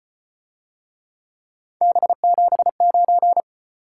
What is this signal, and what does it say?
Morse code, heard as a keyed tone, about 33 wpm, 700 Hz. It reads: B79